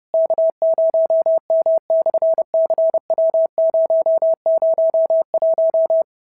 Morse code, heard as a keyed tone, about 30 wpm, 690 Hz. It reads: K0M/CW001
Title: SOTA QSO Practice 30wpm